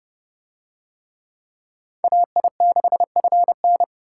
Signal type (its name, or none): Morse code